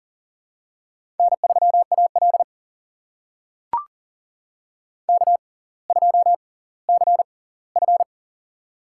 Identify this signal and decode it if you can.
Morse code, heard as a keyed tone, about 39 wpm, spaced as for 20 wpm, 700 Hz. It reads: N3AL K2CF
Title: USA Call Signs - 4 Characters - 6x Character Spacing - 40wpm